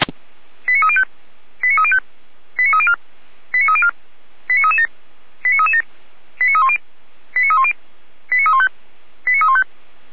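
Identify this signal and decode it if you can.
Five-tone selective call (ZVEI), heard as quick runs of five tones, tones of about 70 ms. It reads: 89305 89305 89304 89304 89337 89337 89310 89310 89315 89315
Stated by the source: Received on an SDR; ZVEI five-tone code 89270.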